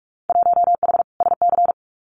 Morse code, heard as a keyed tone, about 46 wpm, 700 Hz. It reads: 1H SC
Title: Winter Field Day Exchanges 45wpm